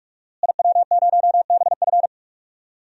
Morse code, heard as a keyed tone, about 43 wpm, 700 Hz. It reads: IW0BF
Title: DXCC-100 Call Signs - 5 Characters - Repeated 2x - 45wpm